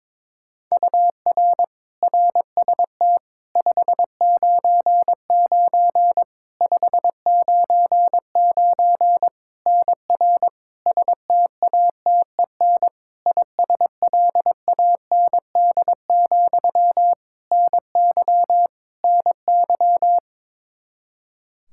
Morse code, heard as a keyed tone, about 22 wpm, 700 Hz. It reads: UR RST 599 599 NR STATEN ISLAND, NY NY
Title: Practice QSO - Exchange 1 - Beginning - 22wpm